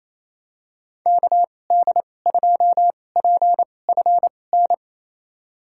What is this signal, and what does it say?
Morse code, heard as a keyed tone, about 28 wpm, spaced as for 20 wpm, 700 Hz. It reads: KD2PFN
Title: USA Call Signs - 6 Characters - 2x Character Spacing - 28wpm